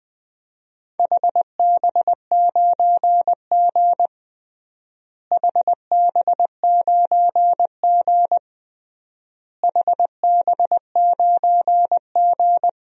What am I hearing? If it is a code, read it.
Morse code, heard as a keyed tone, about 21 wpm, 690 Hz. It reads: HB9G HB9G HB9G
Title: DXCC-100 Call Signs - 4 Characters - Repeated 3x - 20wpm